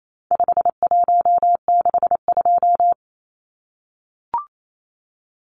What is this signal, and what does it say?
Morse code, heard as a keyed tone, about 28 wpm, 700 Hz. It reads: 5162
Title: Numbers - 4-Digit 28wpm